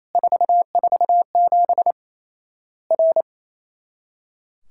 Morse code, heard as a keyed tone, about 28 wpm, 700 Hz. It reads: 447 R
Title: Worked All Europe DX Contest - QTC Exchange - 28wpm